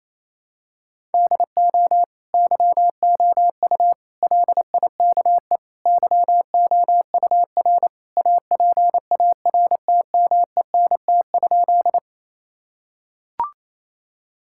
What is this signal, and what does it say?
Morse code, heard as a keyed tone, about 27 wpm, 700 Hz. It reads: DO YOU LIKE YOUR APARTMENT?
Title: Sentences from Top 1000 Words - Review Part 6 - 28wpm